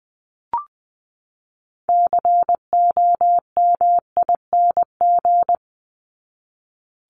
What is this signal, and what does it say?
Morse code, heard as a keyed tone, about 20 wpm, 700 Hz. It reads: COMING
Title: ING Suffix - Top 100 Words with Base Words 20wpm